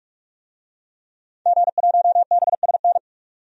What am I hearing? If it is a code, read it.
Morse code, heard as a keyed tone, about 46 wpm, 700 Hz. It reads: G1BSN
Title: DXCC-100 Call Signs - 5 Characters - 45wpm